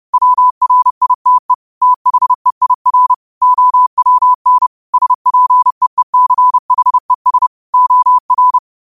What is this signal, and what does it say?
Morse code, heard as a keyed tone, about 30 wpm, 1000 Hz. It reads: WRITE THEIR OWN SPEECHES OR